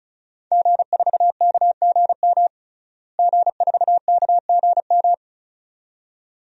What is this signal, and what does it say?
Morse code, heard as a keyed tone, about 35 wpm, 700 Hz. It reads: G4KGM G4KGM
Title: DXCC-100 Call Signs - 5 Characters - Repeated 2x - 35wpm